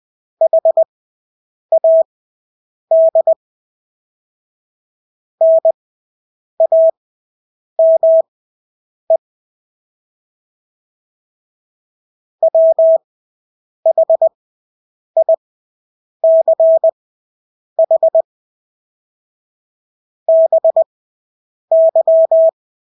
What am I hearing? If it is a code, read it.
Morse code, heard as a keyed tone, about 20 wpm, spaced as for 8 wpm, 650 Hz. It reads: HAD NAME WHICH BY
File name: Engelske ord 20/8 wpm